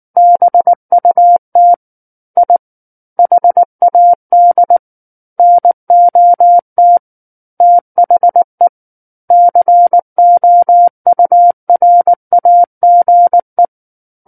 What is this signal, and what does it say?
Morse code, heard as a keyed tone, about 19 wpm, 700 Hz. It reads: BUT I HAD NOT THE COURAGE
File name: Book One / 10-Chapter 10 - 19 WPM